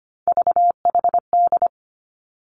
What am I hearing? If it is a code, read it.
Morse code, heard as a keyed tone, about 25 wpm, 700 Hz. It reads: VHD
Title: Three Challenging Letter-Number Combinations 25wpm